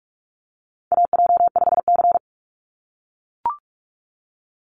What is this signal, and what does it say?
Morse code, heard as a keyed tone, about 44 wpm, 700 Hz. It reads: AJ5C
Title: NA SOTA Chaser Callsigns 45wpm